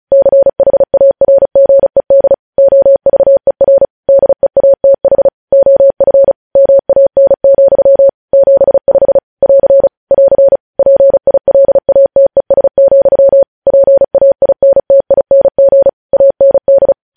Morse code, heard as a keyed tone, about 36 wpm, 550 Hz. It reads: CHARGED OVER DEATH OF MAN, 75 <AR> <AR> PILATES, PAINTING AND